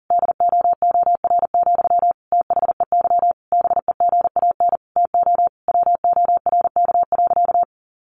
Morse code, heard as a keyed tone, about 40 wpm, 700 Hz. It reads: DOOR, THEY BEGAN TO WORK.
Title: Sentences from Top 1400 Words 40wpm